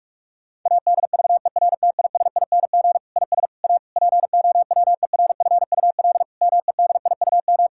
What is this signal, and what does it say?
Morse code, heard as a keyed tone, about 46 wpm, 700 Hz. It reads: ADVERTISING IS A POWERFUL MEDIUM
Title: Sentences from Top 1000 Words - Review Part 6 - 45wpm